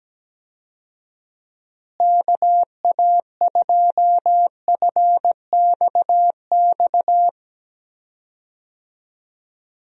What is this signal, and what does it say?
Morse code, heard as a keyed tone, about 17 wpm, 700 Hz. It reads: KA2FXX
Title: USA Call Signs - 6 Characters - Repeated 3x - 17wpm